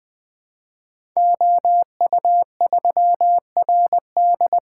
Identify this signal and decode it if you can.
Morse code, heard as a keyed tone, about 20 wpm, 700 Hz. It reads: OU3RD
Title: Code Groups - 5 Letter:Numbers - 20wpm